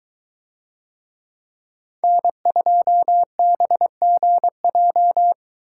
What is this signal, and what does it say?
Morse code, heard as a keyed tone, about 23 wpm, 700 Hz. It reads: N2BGJ